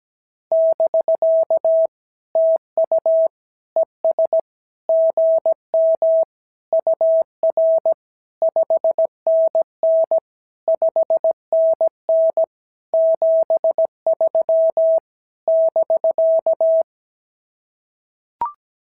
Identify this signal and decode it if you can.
Morse code, heard as a keyed tone, about 17 wpm, 650 Hz. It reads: <BK> TU ES GM UR 5NN 5NN 73 <BK>
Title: SOTA QSO Practice 17wpm